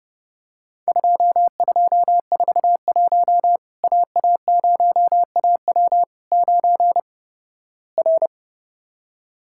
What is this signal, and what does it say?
Morse code, heard as a keyed tone, about 30 wpm, 700 Hz. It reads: 2241 AA0AW 9 R
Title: Worked All Europe DX Contest - QTC Exchange - 30wpm